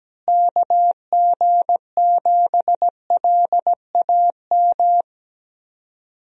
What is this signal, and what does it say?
Morse code, heard as a keyed tone, about 17 wpm, 700 Hz. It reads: KG7LAM
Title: USA Call Signs - 6 Characters - Repeated 3x - 17wpm